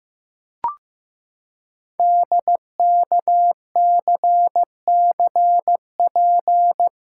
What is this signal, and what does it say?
Morse code, heard as a keyed tone, about 15 wpm, 700 Hz. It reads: DKCCP
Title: Code Groups - 5 Letters - 15wpm